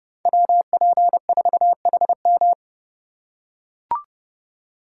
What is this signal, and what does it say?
Morse code, heard as a keyed tone, about 30 wpm, 700 Hz. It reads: WP4HM